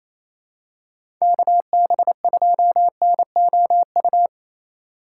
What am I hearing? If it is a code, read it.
Morse code, heard as a keyed tone, about 28 wpm, 700 Hz. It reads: KB2NOU
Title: USA Call Signs - 6 Characters - 2x Character Spacing - 28wpm